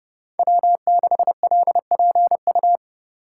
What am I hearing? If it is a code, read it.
Morse code, heard as a keyed tone, about 30 wpm, 700 Hz. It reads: W6LPU
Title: USA Call Signs - 3x Character Spacing - 30wpm